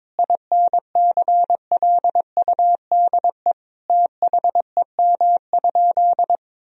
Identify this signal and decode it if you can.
Morse code, heard as a keyed tone, about 21 wpm, 700 Hz. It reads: INCLUDE THEM?
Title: Sentences from Top 1000 Words - Review Part 3 - 22wpm